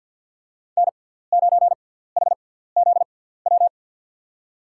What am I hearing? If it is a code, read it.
Morse code, heard as a keyed tone, about 50 wpm, spaced as for 17 wpm, 700 Hz. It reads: N9HZW
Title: USA Call Signs - 5 Characters - 6x Character Spacing - 50wpm